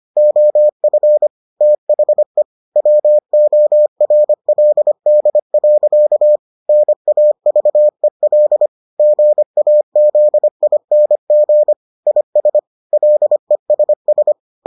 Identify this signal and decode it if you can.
Morse code, heard as a keyed tone, about 25 wpm, 600 Hz. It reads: OF THE WORLD. NAVEL GAZING IS LESS